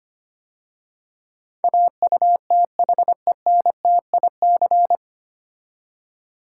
Morse code, heard as a keyed tone, about 25 wpm, 700 Hz. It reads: AUTHENTIC
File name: practice / 25wpm / 9 Letter Words - First 3 Letters Given - 25wpm